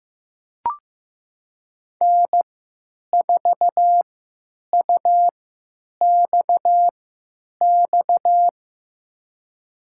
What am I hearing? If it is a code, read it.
Morse code, heard as a keyed tone, about 15 wpm, spaced as for 9 wpm, 700 Hz. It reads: N4UXX
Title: USA Call Signs - 3x Character Spacing - 15wpm